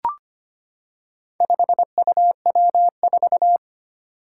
Morse code, heard as a keyed tone, about 25 wpm, 700 Hz. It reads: 5UW4
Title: Four Challenging Letter-Number Combinations 25wpm